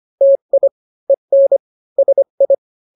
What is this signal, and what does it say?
Morse code, heard as a keyed tone, about 25 wpm, 550 Hz. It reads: TI EN SI